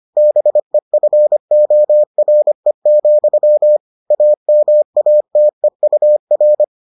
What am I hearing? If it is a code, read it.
Morse code, heard as a keyed tone, about 25 wpm, 600 Hz. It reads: BEFORE, AMATEUR